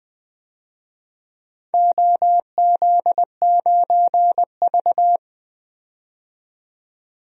Morse code, heard as a keyed tone, about 21 wpm, 700 Hz. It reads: OZ9V
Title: DXCC-100 Call Signs - 4 Characters - 20wpm